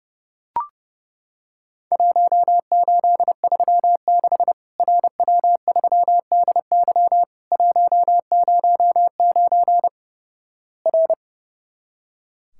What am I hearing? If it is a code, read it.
Morse code, heard as a keyed tone, about 31 wpm, 700 Hz. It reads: 1836 RW3DY 109 R
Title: Worked All Europe DX Contest - QTC Exchange - 30wpm